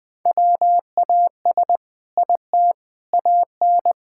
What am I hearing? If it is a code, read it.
Morse code, heard as a keyed tone, about 20 wpm, 700 Hz. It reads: WAS IT AN
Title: Sentences from Top 1000 Words - Review Part 6 - 20wpm